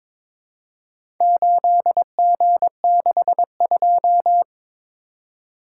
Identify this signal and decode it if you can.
Morse code, heard as a keyed tone, about 22 wpm, 700 Hz. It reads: 8G62
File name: practice / 22wpm / Four Letter-Number Combinations - 22wpm